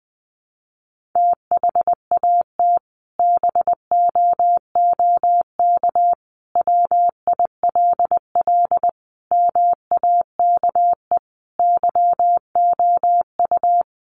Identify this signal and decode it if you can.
Morse code, heard as a keyed tone, about 20 wpm, 700 Hz. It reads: THAT BOOK WILL MAKE YOU
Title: Sentences from Top 1500 Words 20wpm